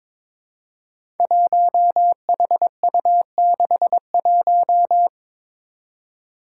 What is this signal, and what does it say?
Morse code, heard as a keyed tone, about 23 wpm, 700 Hz. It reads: 1HU61
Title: Five Challenging Letter-Number Combinations 22wpm